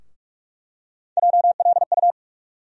Morse code, heard as a keyed tone, about 46 wpm, 700 Hz. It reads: JLU
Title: USA Call Sign Postfixes - 3 Letters 45wpm